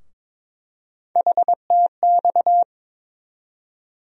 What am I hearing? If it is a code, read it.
Morse code, heard as a keyed tone, about 22 wpm, 700 Hz. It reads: HTX
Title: USA Call Sign Postfixes - 3 Letters 22wpm